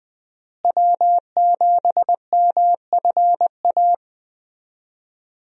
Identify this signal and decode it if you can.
Morse code, heard as a keyed tone, about 20 wpm, 700 Hz. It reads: W7MFA
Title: USA Call Signs - 5 Characters - Repeated 2x - 20wpm